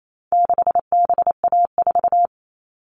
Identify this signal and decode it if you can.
Morse code, heard as a keyed tone, about 28 wpm, 700 Hz. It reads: 6BA4
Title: All Cut Numbers - 4-Digit 28wpm